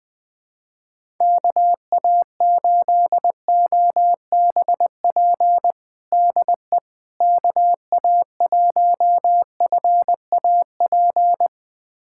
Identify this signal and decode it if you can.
Morse code, heard as a keyed tone, about 20 wpm, 700 Hz. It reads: KA8OBP DE KA1FAP
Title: USA Callsign DE Callsign - 6 Characters - 20wpm